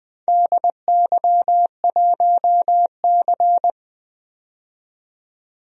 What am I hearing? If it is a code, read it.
Morse code, heard as a keyed tone, about 20 wpm, 700 Hz. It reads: DY1C
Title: DXCC-100 Call Signs - 4 Characters - 20wpm